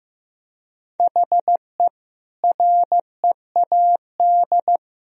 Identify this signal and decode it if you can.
Morse code, heard as a keyed tone, about 15 wpm, 700 Hz. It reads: HE READ
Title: Sentences from Top 1000 Words - Review Part 1 - 15wpm